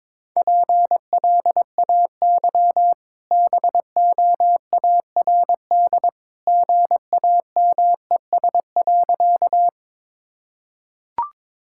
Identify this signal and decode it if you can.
Morse code, heard as a keyed tone, about 22 wpm, 690 Hz. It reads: PLAY BOARD GAMES.
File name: practice / 22wpm / Easy Sentences with Hints - 22wpm